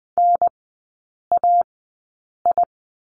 Morse code, heard as a keyed tone, about 20 wpm, spaced as for 8 wpm, 700 Hz. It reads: NAI
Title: 041 Lesson 8e - Number 1 - Three Characters 20wpm